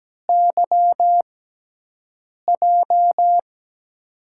Morse code, heard as a keyed tone, about 17 wpm, spaced as for 6 wpm, 700 Hz. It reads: YJ
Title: USA Call Signs - 6 Characters - 6x Character Spacing - 17wpm